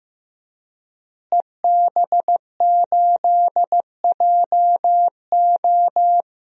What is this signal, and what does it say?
Morse code, heard as a keyed tone, about 15 wpm, 700 Hz. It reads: EB8JO
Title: DXCC-100 Call Signs - 5 Characters - Repeated 2x - 15wpm